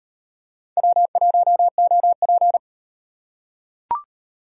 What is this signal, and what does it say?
Morse code, heard as a keyed tone, about 33 wpm, 700 Hz. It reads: W1OP